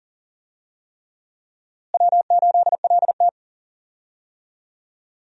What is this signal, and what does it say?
Morse code, heard as a keyed tone, about 40 wpm, 700 Hz. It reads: W8LT
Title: USA Call Signs - 4 Characters - Repeated 3x - 40wpm